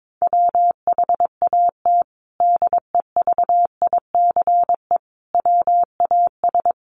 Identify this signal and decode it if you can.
Morse code, heard as a keyed tone, about 22 wpm, 700 Hz. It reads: WHAT DEVICE WAS
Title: Sentences from Top 1500 Words 22wpm